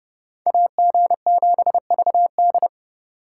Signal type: Morse code

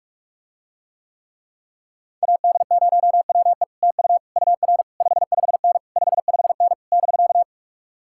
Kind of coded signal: Morse code

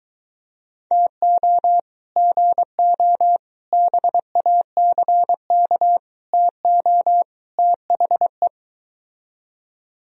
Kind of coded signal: Morse code

